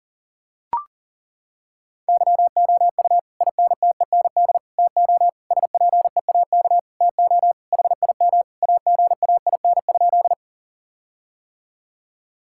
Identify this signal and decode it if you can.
Morse code, heard as a keyed tone, about 40 wpm, 700 Hz. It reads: YOU INTEND TO SPEAK TO HIM AGAIN?
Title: Sentences from Top 1600 Words 40wpm